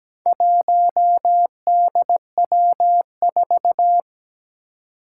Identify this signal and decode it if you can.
Morse code, heard as a keyed tone, about 17 wpm, 700 Hz. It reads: 1DW4
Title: Four Challenging Letter-Number Combinations 17wpm